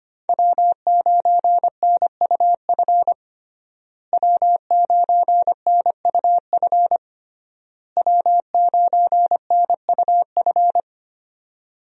Morse code, heard as a keyed tone, about 25 wpm, 700 Hz. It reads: W9NUF W9NUF W9NUF
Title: USA Call Signs - 5 Characters - Repeated 3x - 25wpm